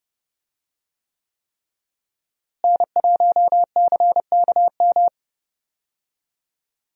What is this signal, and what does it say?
Morse code, heard as a keyed tone, about 30 wpm, 700 Hz. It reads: N1CKM
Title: USA Call Signs - 5 Characters - Repeated 3x - 30wpm